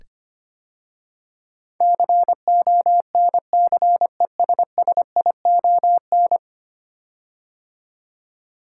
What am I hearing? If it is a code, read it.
Morse code, heard as a keyed tone, about 25 wpm, 700 Hz. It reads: CONCESSION
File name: 10 Letter Words - First 3 Letters Given - 25wpm